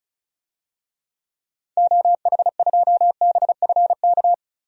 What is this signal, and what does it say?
Morse code, heard as a keyed tone, about 34 wpm, 700 Hz. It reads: OH2BFK
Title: DXCC-100 Call Signs - 6 Characters - 35wpm